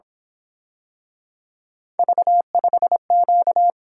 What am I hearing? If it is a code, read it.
Morse code, heard as a keyed tone, about 26 wpm, 700 Hz. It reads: V5Q
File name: Speed-Racing - Three Letter-Number Combinations - 22wpm